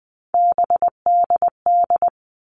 Morse code, heard as a keyed tone, about 20 wpm, 700 Hz. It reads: BDD
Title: All Cut Numbers - 3-Digit 20wpm